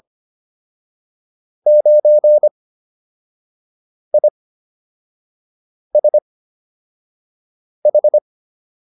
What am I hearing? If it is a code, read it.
Morse code, heard as a keyed tone, about 25 wpm, 600 Hz. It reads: 9 I S H